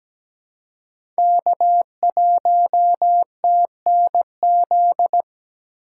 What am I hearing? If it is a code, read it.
Morse code, heard as a keyed tone, about 17 wpm, 700 Hz. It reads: K1TNZ
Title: USA Call Signs - 5 Characters - 3x Character Spacing - 17wpm